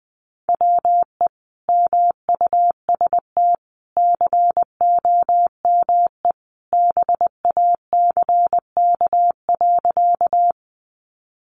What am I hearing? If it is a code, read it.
Morse code, heard as a keyed tone, about 20 wpm, 700 Hz. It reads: WE MUST COME BACK.
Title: Sentences from Top 200 Words 20wpm